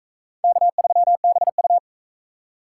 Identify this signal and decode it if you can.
Morse code, heard as a keyed tone, about 39 wpm, 700 Hz. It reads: K3BU